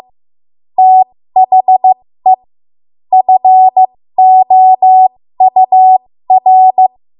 Morse code, heard as a keyed tone, about 15 wpm, spaced as for 13 wpm, 750 Hz. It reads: THE FOUR
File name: Morse / 13WPM